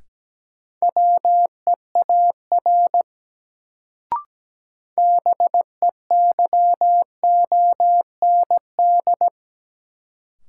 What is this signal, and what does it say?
Morse code, heard as a keyed tone, about 17 wpm, 700 Hz. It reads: WEAR BEYOND
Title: Sentences from Top 1000 Words - Review Part 6 - New Word Drill - 17wpm